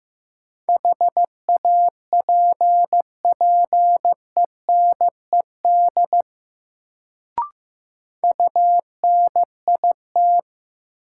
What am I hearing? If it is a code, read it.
Morse code, heard as a keyed tone, about 15 wpm, 700 Hz. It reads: HAPPENED UNIT
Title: Sentences from Top 1000 Words - Review Part 2 - New Word Drill - 15wpm